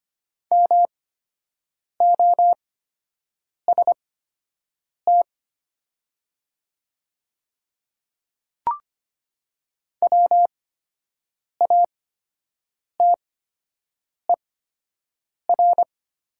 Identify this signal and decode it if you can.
Morse code, heard as a keyed tone, about 25 wpm, spaced as for 7 wpm, 700 Hz. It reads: MOST WATER
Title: 8x Character Spacing - Top 100 Words - 25wpm